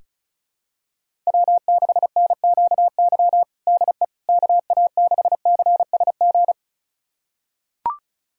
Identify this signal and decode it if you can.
Morse code, heard as a keyed tone, about 34 wpm, 700 Hz. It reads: W6NQY DE KA6CSG
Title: USA Callsign DE Callsign 35wpm